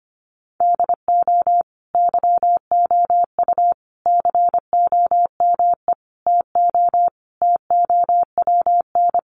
Morse code, heard as a keyed tone, about 25 wpm, 700 Hz. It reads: DO YOU COME TO TOWN
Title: Sentences from Top 1400 Words 25wpm